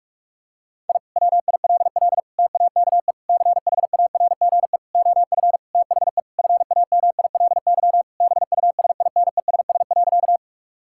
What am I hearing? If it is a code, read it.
Morse code, heard as a keyed tone, about 43 wpm, 700 Hz. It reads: I WILL TAKE CHARGE OF THE FAMILY BUSINESS.
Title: Sentences from Top 1000 Words - Review Part 6 - 45wpm